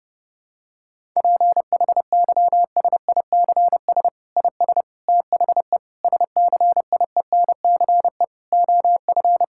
Morse code, heard as a keyed tone, about 32 wpm, 700 Hz. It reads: PHYSICS IS THE SCIENCE OF
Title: Sentences from Top 1000 Words - Review Part 5 - 30wpm